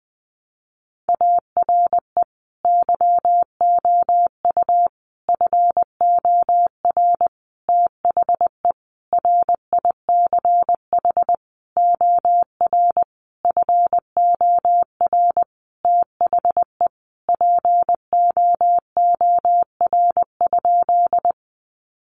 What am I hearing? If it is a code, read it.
Morse code, heard as a keyed tone, about 20 wpm, 700 Hz. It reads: ARE YOU FOR THE RICH OR FOR THE POOR?
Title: Sentences from Top 1100 Words 20wpm